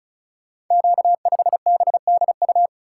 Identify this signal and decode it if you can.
Morse code, heard as a keyed tone, about 34 wpm, 700 Hz. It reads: Q5BDU
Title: Code Groups - 5 Letter:Numbers - 35wpm